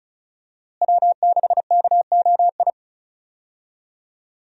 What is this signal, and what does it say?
Morse code, heard as a keyed tone, about 36 wpm, 700 Hz. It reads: W6KOI
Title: USA Call Signs - 5 Characters - Repeated 3x - 35wpm